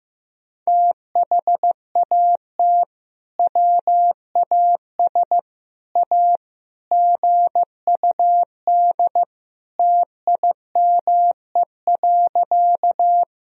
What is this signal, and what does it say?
Morse code, heard as a keyed tone, about 15 wpm, 700 Hz. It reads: THAT WAS A GUD TIME.
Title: Sentences from Top 200 Words and Abbreviations 15wpm